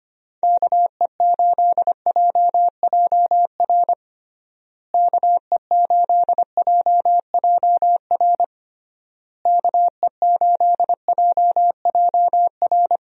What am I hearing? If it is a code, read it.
Morse code, heard as a keyed tone, about 25 wpm, 700 Hz. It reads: KE8JJR KE8JJR KE8JJR
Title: USA Call Signs - 6 Characters - Repeated 3x - 25wpm